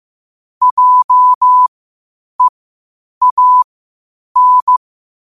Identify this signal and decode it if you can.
Morse code, heard as a keyed tone, about 15 wpm, spaced as for 8 wpm, 1000 Hz. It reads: JEAN
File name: qotdfiles / 8wpm